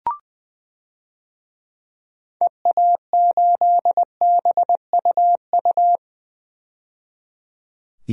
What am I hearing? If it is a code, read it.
Morse code, heard as a keyed tone, about 20 wpm, 700 Hz. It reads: EA8BUU
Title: DXCC-100 Call Signs - 6 Characters - 20wpm